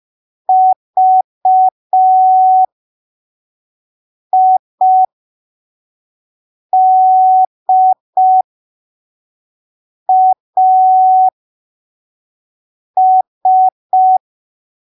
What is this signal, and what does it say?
Morse code, heard as a keyed tone, about 5 wpm, spaced as for 3 wpm, 750 Hz. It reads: VIDAS
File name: arquivos / texto / texto 03 5ppm